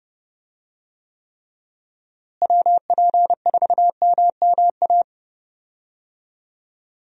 Morse code, heard as a keyed tone, about 30 wpm, 700 Hz. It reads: WP4MMA
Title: USA Call Signs - 6 Characters - Repeated 2x - 30wpm